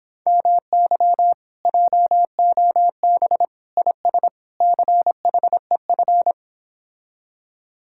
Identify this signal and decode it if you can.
Morse code, heard as a keyed tone, about 26 wpm, 700 Hz. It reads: MY JOB IS CHEF